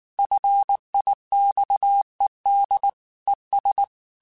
Morse code, heard as a keyed tone, about 19 wpm, 800 Hz. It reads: FIXED ES
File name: QSOs / QSOs-mp3s / WA7QOX 1@18